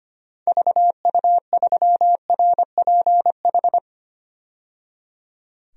Morse code, heard as a keyed tone, about 25 wpm, 700 Hz. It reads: VU3RPH
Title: DXCC-100 Call Signs - 6 Characters - 25wpm